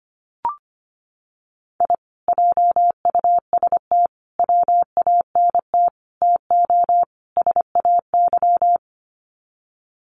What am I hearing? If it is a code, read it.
Morse code, heard as a keyed tone, about 25 wpm, 700 Hz. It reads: I JUST WANT TO SAY
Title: Sets of 5 Words Encore 25wpm